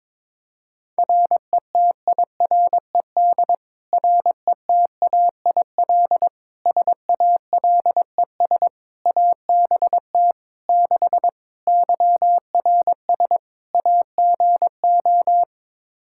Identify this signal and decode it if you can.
Morse code, heard as a keyed tone, about 22 wpm, 700 Hz. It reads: RETIRED RETAIL SALES ABT 6 YRS AGO
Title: Practice QSO - Exchange 3 - More Info - 22wpm